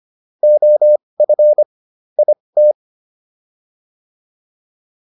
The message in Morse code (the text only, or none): OF IT